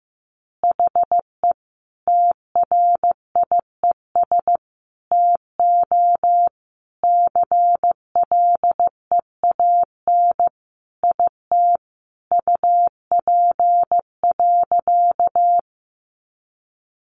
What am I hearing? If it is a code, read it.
Morse code, heard as a keyed tone, about 15 wpm, 700 Hz. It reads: HE TRIES TO CLEAN IT UP.
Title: Additional Easy Phrases 15wpm